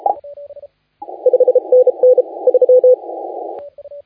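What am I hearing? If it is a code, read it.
Morse code, heard as a keyed tone, about 36 wpm, 500 Hz. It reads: 5 NN 3
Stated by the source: Morse contest audio recorded by LA4C during the SAC CW 2018 contest.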